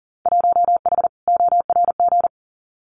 Morse code, heard as a keyed tone, about 38 wpm, 700 Hz. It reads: 1H ORG
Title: Winter Field Day Exchanges 40wpm